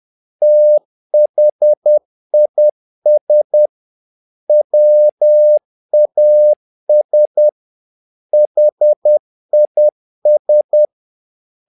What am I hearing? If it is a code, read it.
Morse code, heard as a keyed tone, about 10 wpm, 600 Hz. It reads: THIS WAS HIS
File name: learning center / tower trouble / tower trouble 10wpm